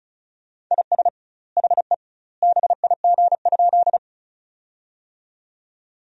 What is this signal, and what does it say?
Morse code, heard as a keyed tone, about 36 wpm, 700 Hz. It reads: IS HE BIG?
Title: 2x Word-Spacing - Sentences from Top 200 Words - 35wpm